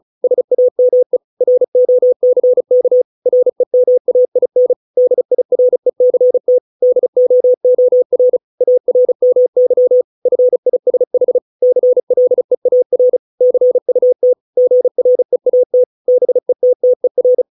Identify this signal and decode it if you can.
Morse code, heard as a keyed tone, about 34 wpm, 500 Hz. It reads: SAME ROCK REMAIN DIRECT DOOR ARMY FISH CLEAR CUT GREAT BETTER